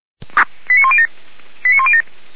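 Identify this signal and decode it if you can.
Five-tone selective call (ZVEI), heard as quick runs of five tones, tones of about 70 ms. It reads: 89107 89107
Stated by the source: Five-tone ZVEI selective call received on an SDR.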